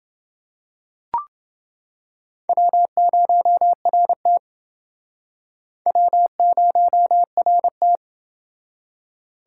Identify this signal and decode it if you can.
Morse code, heard as a keyed tone, about 28 wpm, 700 Hz. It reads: W0RT W0RT